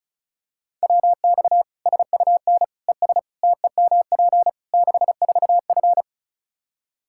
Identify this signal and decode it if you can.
Morse code, heard as a keyed tone, about 34 wpm, 700 Hz. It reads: WX SUN ES TEMP 64F
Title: Practice QSO - Exchange 2 - Info - 35wpm